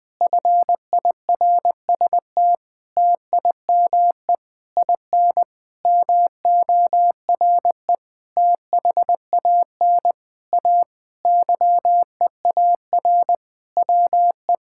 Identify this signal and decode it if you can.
Morse code, heard as a keyed tone, about 20 wpm, 700 Hz. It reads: FIRST TIME IN MORE THAN A YEAR WE
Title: Sentences with 7-Word Phrases - Spoken Beginning - 20wpm